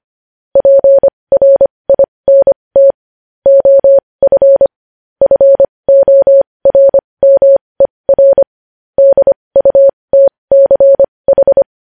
Morse code, heard as a keyed tone, about 25 wpm, spaced as for 20 wpm, 550 Hz. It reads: PRINT OF FORMER DUTCH
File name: podcast / News headlines-1x-25x20